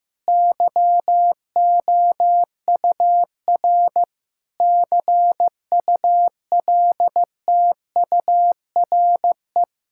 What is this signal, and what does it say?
Morse code, heard as a keyed tone, about 14 wpm, 700 Hz. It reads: YOUR CULTURE
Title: Sentences from Top 1000 Words - Review Part 6 - 15wpm